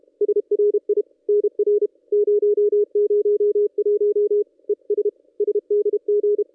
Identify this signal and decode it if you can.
Morse code, heard as a keyed tone, about 32 wpm, 410 Hz. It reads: SRI NR 001 ES SDG
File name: contests / ss-cw-k5tr-op-n5rz -1800